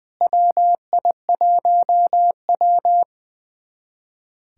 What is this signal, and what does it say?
Morse code, heard as a keyed tone, about 20 wpm, 700 Hz. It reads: WI1W
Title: USA Callsign DE Callsign - 4 Characters - 20wpm